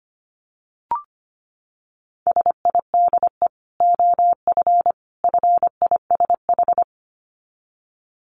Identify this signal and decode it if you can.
Morse code, heard as a keyed tone, about 25 wpm, 700 Hz. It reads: SIDE OF FISH
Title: Dit Challenge 25wpm